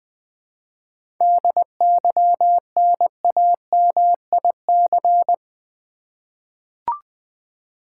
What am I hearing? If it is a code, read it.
Morse code, heard as a keyed tone, about 20 wpm, 700 Hz. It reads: DYNAMIC
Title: 7 Letter Words - First 3 Letters Given - 20wpm